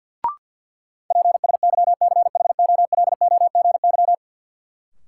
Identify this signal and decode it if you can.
Morse code, heard as a keyed tone, about 48 wpm, 700 Hz. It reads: PSYCHOLOGY
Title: Sentences from Top 1400 Words - New Word Drill Alt - 50wpm